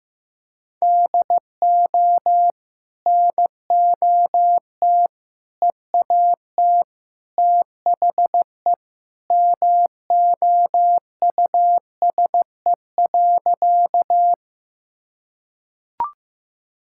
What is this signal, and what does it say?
Morse code, heard as a keyed tone, about 15 wpm, 700 Hz. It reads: DO NOT EAT THE MOUSE.
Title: Sentences from Top 1000 Words - Review Part 5 - 15wpm